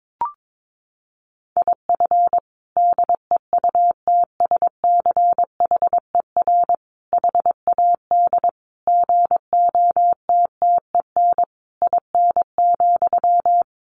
Morse code, heard as a keyed tone, about 22 wpm, 700 Hz. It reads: IF DEUTSCHER HAD GOTTEN IN,
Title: A Sound Of Thunder 22wpm